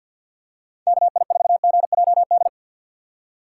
Morse code, heard as a keyed tone, about 51 wpm, 700 Hz. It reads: KI4GJD
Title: USA Call Signs - 6 Characters - 6x Character Spacing - 50wpm